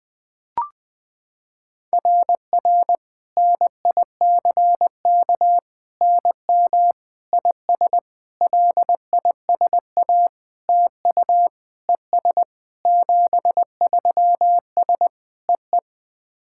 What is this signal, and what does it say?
Morse code, heard as a keyed tone, about 21 wpm, 700 Hz. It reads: RR NICK NM IS LISA TU ES 73S EE
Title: SOTA QSO Practice 20wpm